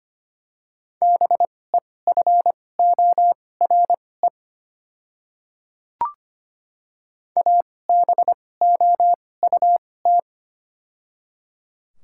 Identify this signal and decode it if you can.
Morse code, heard as a keyed tone, about 25 wpm, spaced as for 18 wpm, 700 Hz. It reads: BEFORE ABOUT
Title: 2x Character Spacing - Top 200 Words - 25wpm